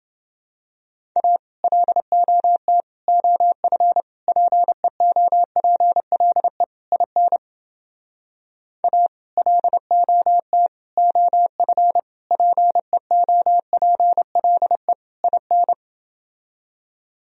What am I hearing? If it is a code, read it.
Morse code, heard as a keyed tone, about 29 wpm, 700 Hz. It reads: A LOT OF PEOPLE IN A LOT OF PEOPLE IN